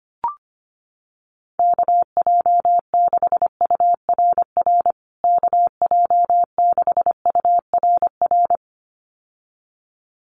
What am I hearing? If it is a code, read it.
Morse code, heard as a keyed tone, about 25 wpm, 700 Hz. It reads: KJ6URR KJ6URR
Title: USA Call Signs Repeated 2x 25wpm